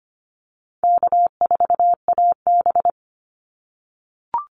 Morse code, heard as a keyed tone, about 25 wpm, 700 Hz. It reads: K4AB
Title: Easy Call Signs - 25wpm